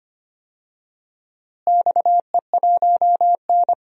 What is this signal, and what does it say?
Morse code, heard as a keyed tone, about 25 wpm, 700 Hz. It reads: XE1N
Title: DXCC-100 Call Signs - 4 Characters - Repeated 2x - 25wpm